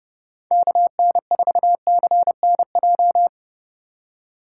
Morse code, heard as a keyed tone, about 31 wpm, 700 Hz. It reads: KN4CNJ
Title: USA Call Signs - 6 Characters - 2x Character Spacing - 30wpm